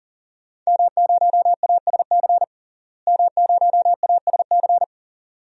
Morse code, heard as a keyed tone, about 40 wpm, 700 Hz. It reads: M0ASC M0ASC
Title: DXCC-100 Call Signs - 5 Characters - Repeated 2x - 40wpm